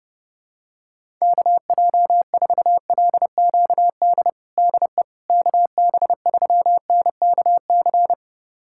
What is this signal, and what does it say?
Morse code, heard as a keyed tone, about 31 wpm, 700 Hz. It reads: KJ4LQD DE KB3NKC
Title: USA Callsign DE Callsign - 6 Characters - 30wpm